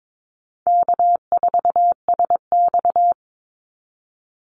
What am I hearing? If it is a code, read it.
Morse code, heard as a keyed tone, about 22 wpm, 700 Hz. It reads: K4SX
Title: Call Signs Repeated 2x - 22wpm